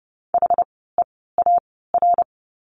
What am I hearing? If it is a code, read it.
Morse code, heard as a keyed tone, about 30 wpm, spaced as for 17 wpm, 700 Hz. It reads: HEAR
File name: course / 30:10wpm / 108 Lesson 18f - Letter m - Words 30wpm